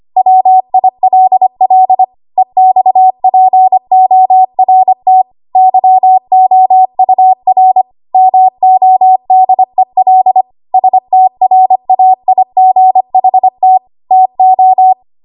Morse code, heard as a keyed tone, about 25 wpm, 750 Hz. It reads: WILL EXPORT YOUR MODEL STRAIGHT TO
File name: Archive / 25 WPM / 25WPM